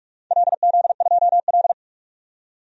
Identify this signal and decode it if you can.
Morse code, heard as a keyed tone, about 46 wpm, 700 Hz. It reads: LZ2L
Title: DXCC-100 Call Signs - 4 Characters - 45wpm